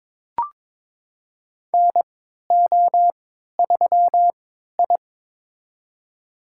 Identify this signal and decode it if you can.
Morse code, heard as a keyed tone, about 22 wpm, spaced as for 12 wpm, 700 Hz. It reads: NO3I
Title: USA Call Signs - 3x Character Spacing - 22wpm